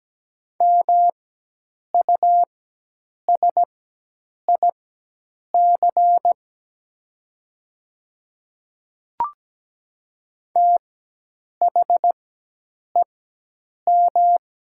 Morse code, heard as a keyed tone, about 17 wpm, spaced as for 8 wpm, 700 Hz. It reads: MUSIC THEM
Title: 4x Character Spacing - Top 300 Words - 17wpm